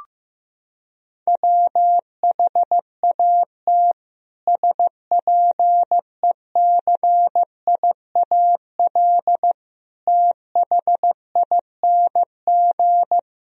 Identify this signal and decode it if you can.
Morse code, heard as a keyed tone, about 14 wpm, 700 Hz. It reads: WHAT SPECIAL THING
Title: Sentences from Top 1000 Words - Review Part 3 - 15wpm